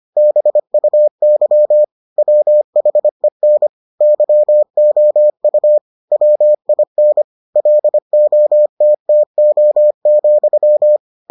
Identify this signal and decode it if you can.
Morse code, heard as a keyed tone, about 25 wpm, 600 Hz. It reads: BUY WHEN YOU WIN LOTTO,